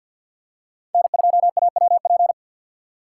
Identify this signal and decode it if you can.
Morse code, heard as a keyed tone, about 53 wpm, 700 Hz. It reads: N2AWP
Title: USA Call Signs - 5 Characters - 6x Character Spacing - 50wpm